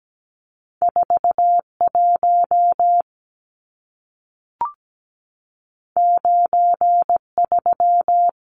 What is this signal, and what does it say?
Morse code, heard as a keyed tone, about 17 wpm, 700 Hz. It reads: 41 93
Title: Numbers - 2-Digit 17wpm